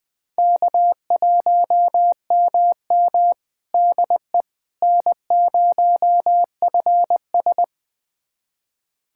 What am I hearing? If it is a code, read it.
Morse code, heard as a keyed tone, about 20 wpm, 700 Hz. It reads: K1MM DE N0FS
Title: USA Callsign DE Callsign - 4 Characters - 20wpm